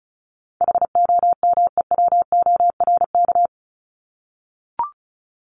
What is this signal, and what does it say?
Morse code, heard as a keyed tone, about 34 wpm, 700 Hz. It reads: HOMEWORK
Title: Top 1501-1750 Words 35wpm